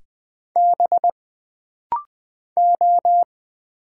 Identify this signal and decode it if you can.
Morse code, heard as a keyed tone, about 20 wpm, 700 Hz. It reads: B O
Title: Single Letters - Rapid-Fire 20wpm